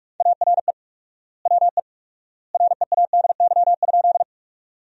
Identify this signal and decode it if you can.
Morse code, heard as a keyed tone, about 41 wpm, 700 Hz. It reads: ARE WE READY?
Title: 4x Word-Spacing - Sentences from Top 400 Words - 45wpm